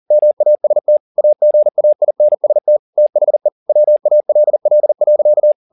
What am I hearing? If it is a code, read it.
Morse code, heard as a keyed tone, about 40 wpm, 600 Hz. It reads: MAST AGAINST THE WALL.
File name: the old man and the sea 2 40wpm